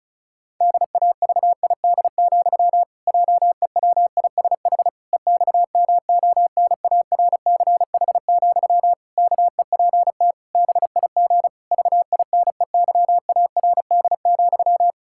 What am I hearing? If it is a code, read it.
Morse code, heard as a keyed tone, about 36 wpm, 700 Hz. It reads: DAVID, JEWISH EXMONARCH, KEPT BIG VINEYARD,
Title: Pangram Challenge v2 - 35wpm